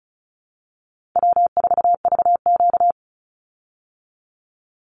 Morse code, heard as a keyed tone, about 34 wpm, 700 Hz. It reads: W4VQ
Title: Easy Call Signs Repeated 2x - 35wpm